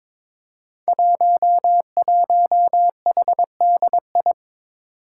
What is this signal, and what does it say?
Morse code, heard as a keyed tone, about 22 wpm, 700 Hz. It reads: 11HDI
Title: Five Challenging Letter-Number Combinations 22wpm